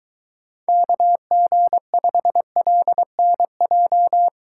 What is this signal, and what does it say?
Morse code, heard as a keyed tone, about 23 wpm, 700 Hz. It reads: KG5LNJ